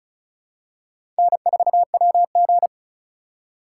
Morse code, heard as a keyed tone, about 36 wpm, 700 Hz. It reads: N4WG
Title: USA Call Signs - 3x Character Spacing - 35wpm